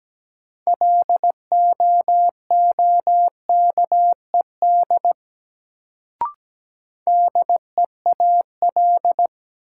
Morse code, heard as a keyed tone, about 17 wpm, 700 Hz. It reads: LOOKED DEAL
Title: Sentences from Top 800 Words - New Word Drill - 17wpm